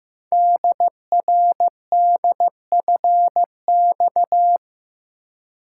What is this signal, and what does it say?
Morse code, heard as a keyed tone, about 15 wpm, 700 Hz. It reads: DRDFX
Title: Code Groups - 5 Letters - 15wpm